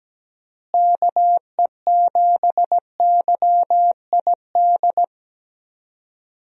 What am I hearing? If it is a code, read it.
Morse code, heard as a keyed tone, about 17 wpm, 700 Hz. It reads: KE7YID
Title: USA Call Signs - 6 Characters - Repeated 3x - 17wpm